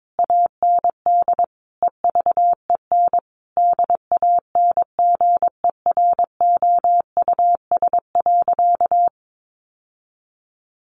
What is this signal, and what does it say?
Morse code, heard as a keyed tone, about 22 wpm, 700 Hz. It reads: AND EVEN DANGEROUS.
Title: Sentences from Top 1200 Words 22wpm